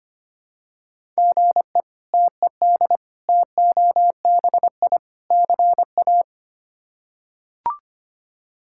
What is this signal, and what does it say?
Morse code, heard as a keyed tone, about 25 wpm, 700 Hz. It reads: GE TED TOBI CA
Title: K1USN SST QSO Practice - 25wpm